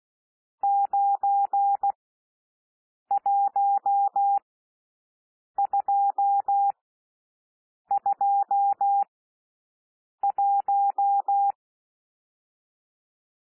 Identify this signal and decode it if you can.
Morse code, heard as a keyed tone, about 16 wpm, spaced as for 6 wpm, 800 Hz. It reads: 91221
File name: SOUND NUMBERS 16 wpm